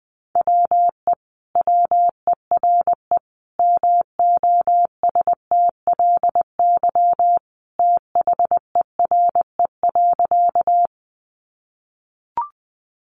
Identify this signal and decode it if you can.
Morse code, heard as a keyed tone, about 19 wpm, 700 Hz. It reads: WE WERE MOSTLY THERE.
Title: Sentences from Top 1300 Words 20wpm